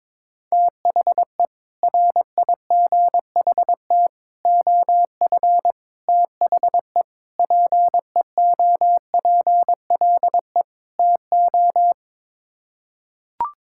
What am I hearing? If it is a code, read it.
Morse code, heard as a keyed tone, about 22 wpm, 700 Hz. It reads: THE RIGHT OF THE PEOPLE TO
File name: practice / 22wpm / Sets of 6 Words - 22wpm